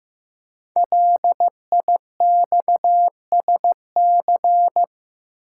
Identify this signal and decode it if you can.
Morse code, heard as a keyed tone, about 15 wpm, 700 Hz. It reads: LIXSC
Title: Code Groups - 5 Letters - 15wpm